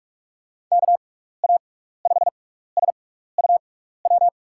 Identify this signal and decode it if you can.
Morse code, heard as a keyed tone, about 42 wpm, spaced as for 15 wpm, 700 Hz. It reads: KA5SUW
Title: USA Call Signs - 6 Characters - 6x Character Spacing - 45wpm